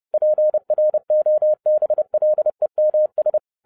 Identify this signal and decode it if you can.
Morse code, heard as a keyed tone, about 30 wpm, 610 Hz. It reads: PROBLEMS